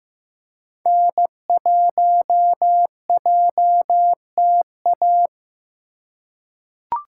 Morse code, heard as a keyed tone, about 15 wpm, 700 Hz. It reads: N1JTA
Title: USA Call Signs - 5 Characters - 4x Character Spacing - 15wpm